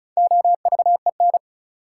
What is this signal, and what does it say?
Morse code, heard as a keyed tone, about 36 wpm, 700 Hz. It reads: OVEN
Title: Sentences from Top 1200 Words - New Word Drill Alt - 35wpm